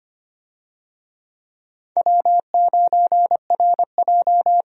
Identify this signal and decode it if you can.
Morse code, heard as a keyed tone, about 25 wpm, 700 Hz. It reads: W9RJ
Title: USA Call Signs - 4 Characters - 25wpm